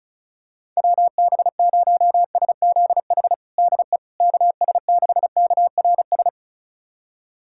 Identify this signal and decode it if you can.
Morse code, heard as a keyed tone, about 34 wpm, 700 Hz. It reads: WB0SZH DE KS6KRS
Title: USA Callsign DE Callsign 35wpm